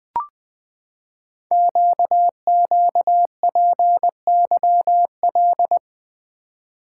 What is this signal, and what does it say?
Morse code, heard as a keyed tone, about 20 wpm, 700 Hz. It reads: QQPYL
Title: Code Groups - 5 Letters - 20wpm